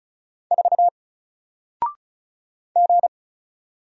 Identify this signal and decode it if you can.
Morse code, heard as a keyed tone, about 36 wpm, 700 Hz. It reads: 4 G
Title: Single Letter-Number 35wpm